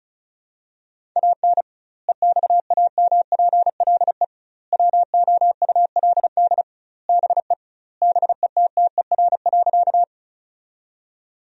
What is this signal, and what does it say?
Morse code, heard as a keyed tone, about 36 wpm, 700 Hz. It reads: AN EXAMPLE WOULD BE BETTER.
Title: 2x Word-Spacing - Sentences from Top 400 Words - 35wpm